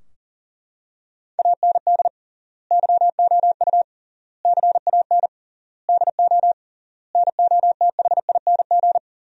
Morse code, heard as a keyed tone, about 40 wpm, 700 Hz. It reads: AND YOU CAN DO NOTHING
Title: 3x Word-Spacing - Sentences from Top 500 Words - 40wpm